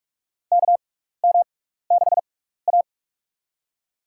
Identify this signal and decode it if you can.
Morse code, heard as a keyed tone, about 41 wpm, spaced as for 15 wpm, 700 Hz. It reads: KM6A
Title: USA Call Signs - 6x Character Spacing - 45wpm